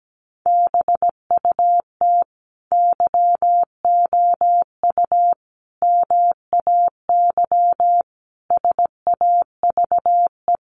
Morse code, heard as a keyed tone, about 17 wpm, 700 Hz. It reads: BUT YOU MAY SAVE
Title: Sentences from Top 1100 Words 17wpm